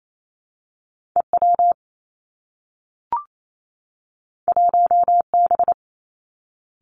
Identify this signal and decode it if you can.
Morse code, heard as a keyed tone, about 28 wpm, 700 Hz. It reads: E W 1 B
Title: Two Challenging Letter-Number Combinations 28wpm